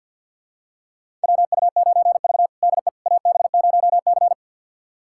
Morse code, heard as a keyed tone, about 47 wpm, 700 Hz. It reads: WU9V DE AB0C
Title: USA Callsign DE Callsign - 4 Characters - 50wpm